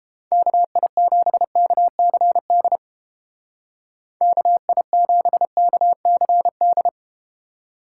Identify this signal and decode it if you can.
Morse code, heard as a keyed tone, about 32 wpm, 700 Hz. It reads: KI7KCD KI7KCD